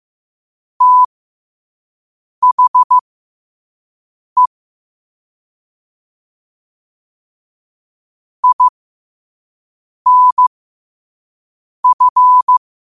Morse code, heard as a keyed tone, about 15 wpm, spaced as for 5 wpm, 1000 Hz. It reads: THE INF
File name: qotdfiles / 5wpm